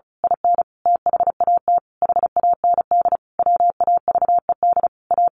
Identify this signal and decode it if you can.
Morse code, heard as a keyed tone, about 34 wpm, 700 Hz. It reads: IN THAT HAND WAVED A
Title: A Sound Of Thunder 35wpm